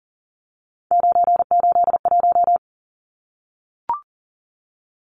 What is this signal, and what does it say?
Morse code, heard as a keyed tone, about 40 wpm, 700 Hz. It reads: 981
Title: Numbers - 3-Digit 40wpm